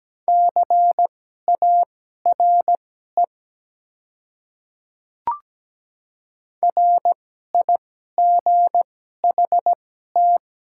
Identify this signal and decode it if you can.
Morse code, heard as a keyed tone, about 17 wpm, spaced as for 12 wpm, 700 Hz. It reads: CARE RIGHT
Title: 2x Character Spacing - Top 300 Words - 17wpm